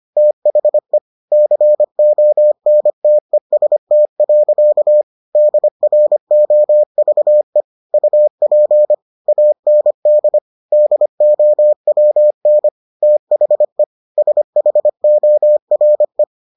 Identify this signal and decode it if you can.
Morse code, heard as a keyed tone, about 25 wpm, 600 Hz. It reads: THE CONTEST. DROVE UP AND DOWN THE SHORE